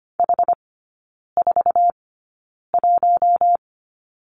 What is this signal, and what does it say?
Morse code, heard as a keyed tone, about 25 wpm, spaced as for 9 wpm, 700 Hz. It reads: H41
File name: course / 25:10wpm / 093 Lesson 16e - Letter c - Three Characters 25wpm